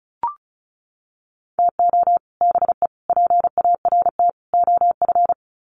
Morse code, heard as a keyed tone, about 34 wpm, 700 Hz. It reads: TO BE PART OF
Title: Sets of 4 Words 35wpm